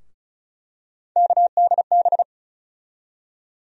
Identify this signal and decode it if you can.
Morse code, heard as a keyed tone, about 36 wpm, 700 Hz. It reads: KDB
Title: USA Call Sign Postfixes - 3 Letters 35wpm